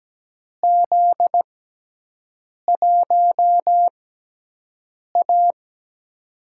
Morse code, heard as a keyed tone, about 17 wpm, spaced as for 6 wpm, 700 Hz. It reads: Z1A
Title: USA Call Signs - 6x Character Spacing - 17wpm